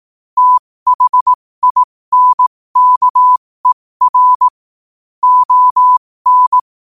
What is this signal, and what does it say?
Morse code, heard as a keyed tone, about 18 wpm, spaced as for 15 wpm, 1000 Hz. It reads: THINKER ON
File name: qotdfiles / 15wpm